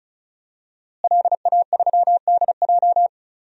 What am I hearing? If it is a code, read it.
Morse code, heard as a keyed tone, about 36 wpm, 700 Hz. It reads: LA3DJ
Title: DXCC-100 Call Signs - 5 Characters - Repeated 2x - 35wpm